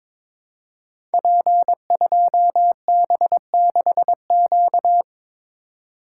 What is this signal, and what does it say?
Morse code, heard as a keyed tone, about 22 wpm, 700 Hz. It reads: P2B6Q
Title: Code Groups - 5 Letter:Numbers - 22wpm